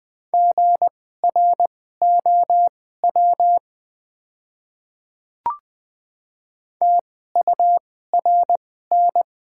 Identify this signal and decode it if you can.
Morse code, heard as a keyed tone, about 20 wpm, spaced as for 14 wpm, 700 Hz. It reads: GROW TURN
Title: 2x Character Spacing - Top 300 Words - 20wpm